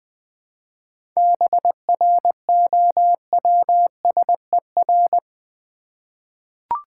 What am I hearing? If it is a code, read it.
Morse code, heard as a keyed tone, about 20 wpm, 700 Hz. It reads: BROWSER
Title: 7 Letter Words - First 3 Letters Given - 20wpm